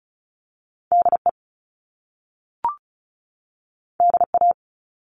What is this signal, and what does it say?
Morse code, heard as a keyed tone, about 35 wpm, 700 Hz. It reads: D E D A
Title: Top Two Letter Combinations in Words 35wpm